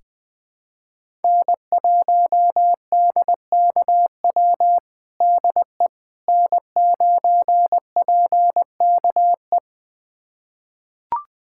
Morse code, heard as a keyed tone, about 20 wpm, 700 Hz. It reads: N1DKW DE N9PKE
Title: USA Callsign DE Callsign - 5 Characters - 20wpm